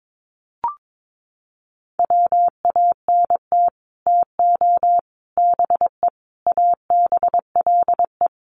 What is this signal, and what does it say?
Morse code, heard as a keyed tone, about 22 wpm, 700 Hz. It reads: WANT TO BE ABLE
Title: Sets of 5 Words 22wpm